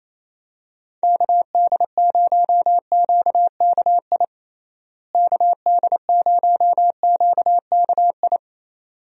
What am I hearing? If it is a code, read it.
Morse code, heard as a keyed tone, about 28 wpm, 700 Hz. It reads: KD0QKI KD0QKI
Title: USA Call Signs - 6 Characters - Repeated 2x - 28wpm